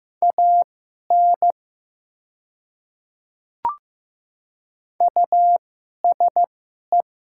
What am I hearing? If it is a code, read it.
Morse code, heard as a keyed tone, about 15 wpm, spaced as for 11 wpm, 700 Hz. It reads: AN USE
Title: 2x Character Spacing - Top 100 Words - 15wpm